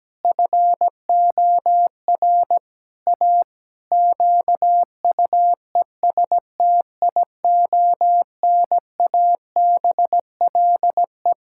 Morse code, heard as a keyed tone, about 17 wpm, 700 Hz. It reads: FOR A QUESTIONABLE
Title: Pangram Challenge v2 - 17wpm